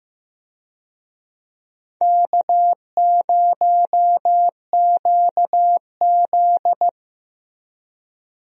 Morse code, heard as a keyed tone, about 15 wpm, 700 Hz. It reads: K0QZ
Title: USA Call Signs - 4 Characters - Repeated 3x - 15wpm